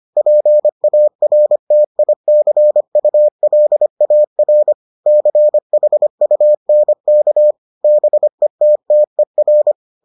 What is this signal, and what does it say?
Morse code, heard as a keyed tone, about 25 wpm, 600 Hz. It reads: PARTICULAR CHUNK BETTER